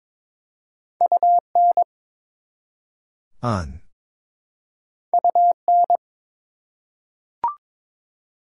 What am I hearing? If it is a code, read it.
Morse code, heard as a keyed tone, about 22 wpm, 700 Hz. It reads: UN UN
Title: Prefixes and Suffixes 22wpm